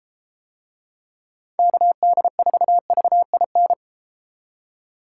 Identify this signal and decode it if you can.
Morse code, heard as a keyed tone, about 30 wpm, 700 Hz. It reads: KD4VIN